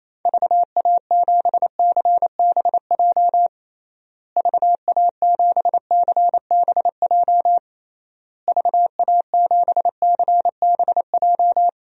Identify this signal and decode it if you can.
Morse code, heard as a keyed tone, about 28 wpm, 700 Hz. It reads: VA7CBJ VA7CBJ VA7CBJ
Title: DXCC-100 Call Signs - 6 Characters - Repeated 3x - 28wpm